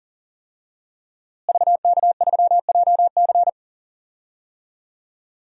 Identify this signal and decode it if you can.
Morse code, heard as a keyed tone, about 38 wpm, 700 Hz. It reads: VK3JC
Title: DXCC-100 Call Signs - 5 Characters - Repeated 2x - 40wpm